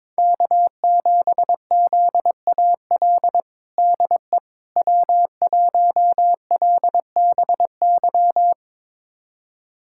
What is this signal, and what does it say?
Morse code, heard as a keyed tone, about 22 wpm, 700 Hz. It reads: K7ZAL DE W1LBY
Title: USA Callsign DE Callsign - 5 Characters - 22wpm